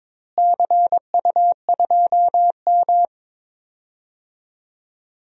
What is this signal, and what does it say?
Morse code, heard as a keyed tone, about 21 wpm, 700 Hz. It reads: CU2M
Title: DXCC-100 Call Signs - 4 Characters - Repeated 2x - 22wpm